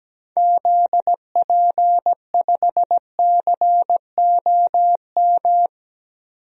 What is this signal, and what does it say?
Morse code, heard as a keyed tone, about 17 wpm, 700 Hz. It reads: ZP5COM
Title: DXCC-100 Call Signs - 6 Characters - 17wpm